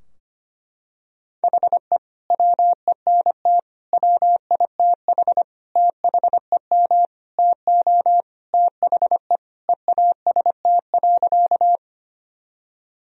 Morse code, heard as a keyed tone, about 25 wpm, 700 Hz. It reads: HE WENT WITH THEM TO THE EAST.
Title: Sentences from Top 1000 Words - Review Part 3 - 25wpm